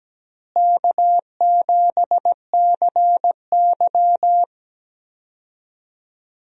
Morse code, heard as a keyed tone, about 17 wpm, 700 Hz. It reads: K7CY
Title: USA Call Signs - 4 Characters - Repeated 3x - 17wpm